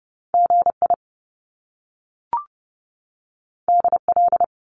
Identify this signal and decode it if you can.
Morse code, heard as a keyed tone, about 30 wpm, 700 Hz. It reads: GI DL
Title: Top Two Letter Combinations in Words 30wpm